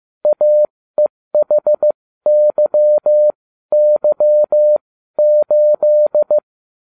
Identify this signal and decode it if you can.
Morse code, heard as a keyed tone, about 15 wpm, spaced as for 13 wpm, 600 Hz. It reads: AEHYY8